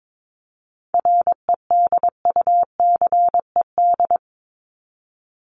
Morse code, heard as a keyed tone, about 22 wpm, 700 Hz. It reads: REDUCED
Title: ED Suffix - Top 100 Words with Base Words 22wpm